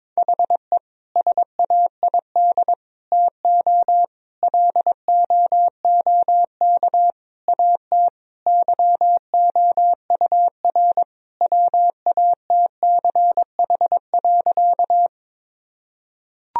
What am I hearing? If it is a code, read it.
Morse code, heard as a keyed tone, about 21 wpm, 700 Hz. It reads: HE SAID TO LOOK AT YOUR WATCH.
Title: Sentences from Top 1000 Words - Review Part 1 - 22wpm